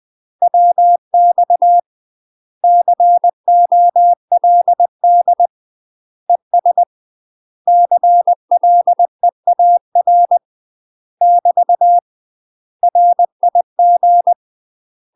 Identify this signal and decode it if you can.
Morse code, heard as a keyed tone, about 20 wpm, 700 Hz. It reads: WX COLD ES CLEAR <BT> RIG
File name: UK MASTER QSOs TRAINING / 20WPM / UK MASTER QSOs 0014